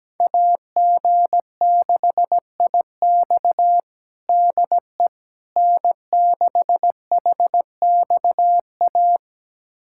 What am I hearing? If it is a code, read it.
Morse code, heard as a keyed tone, about 17 wpm, 700 Hz. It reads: AG6IX DE N6HXA
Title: USA Callsign DE Callsign - 5 Characters - 17wpm